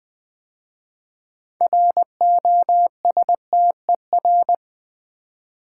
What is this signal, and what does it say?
Morse code, heard as a keyed tone, about 20 wpm, 700 Hz. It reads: ROSTER
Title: 6 Letter Words - First 3 Letters Given - 20wpm